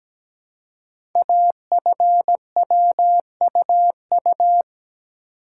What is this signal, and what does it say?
Morse code, heard as a keyed tone, about 17 wpm, 700 Hz. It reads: AFWUU
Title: Code Groups - 5 Letters - 17wpm